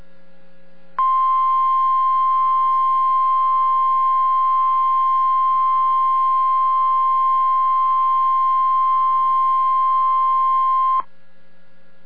Emergency Alert System (SAME) tones (the 1050 Hz attention tone). It is heard from 1.0 to 11.1 s.